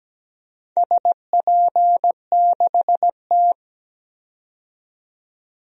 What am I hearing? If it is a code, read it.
Morse code, heard as a keyed tone, about 17 wpm, 700 Hz. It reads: SP6T
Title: DXCC-100 Call Signs - 4 Characters - 17wpm